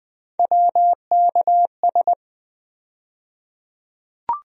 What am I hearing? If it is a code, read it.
Morse code, heard as a keyed tone, about 20 wpm, 700 Hz. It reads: WKS
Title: USA Call Sign Postfixes - 3 Letters 20wpm